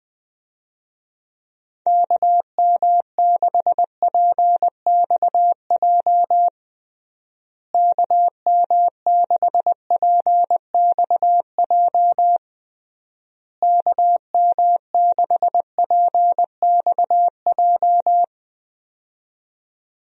Morse code, heard as a keyed tone, about 20 wpm, 700 Hz. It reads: KM6PXJ KM6PXJ KM6PXJ
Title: USA Call Signs - 6 Characters - Repeated 3x - 20wpm